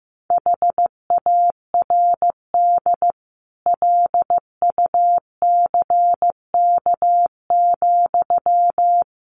Morse code, heard as a keyed tone, about 15 wpm, 700 Hz. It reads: HARD LUCK,
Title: Sentences from Top 1500 Words 15wpm